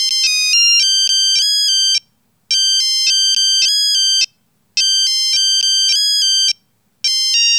There is no code.